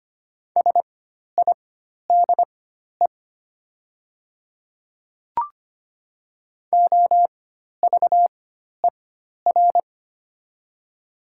Morse code, heard as a keyed tone, about 25 wpm, spaced as for 12 wpm, 690 Hz. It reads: SIDE OVER
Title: 4x Character Spacing - Top 200 Words - 25wpm